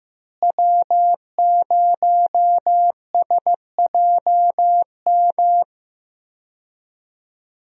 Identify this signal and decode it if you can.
Morse code, heard as a keyed tone, about 15 wpm, 700 Hz. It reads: W0SJM
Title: USA Call Signs - 5 Characters - Repeated 3x - 15wpm